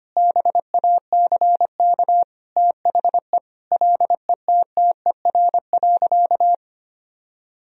Morse code, heard as a keyed tone, about 25 wpm, 700 Hz. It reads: BACK THE LETTER.
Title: Sentences from Top 1000 Words - Review Part 4 - 25wpm